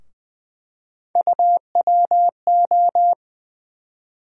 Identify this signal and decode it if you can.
Morse code, heard as a keyed tone, about 20 wpm, 700 Hz. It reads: UWO
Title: USA Call Sign Postfixes - 3 Letters 20wpm